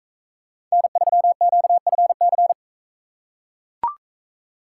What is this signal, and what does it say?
Morse code, heard as a keyed tone, about 42 wpm, 700 Hz. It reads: N3QFC